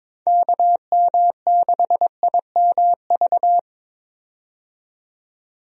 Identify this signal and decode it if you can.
Morse code, heard as a keyed tone, about 23 wpm, 700 Hz. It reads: KM6IMV
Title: USA Call Signs - 6 Characters - 22wpm